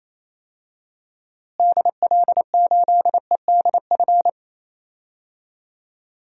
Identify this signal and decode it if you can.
Morse code, heard as a keyed tone, about 28 wpm, 700 Hz. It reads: DL8EDF
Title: DXCC-100 Call Signs - 6 Characters - Repeated 3x - 28wpm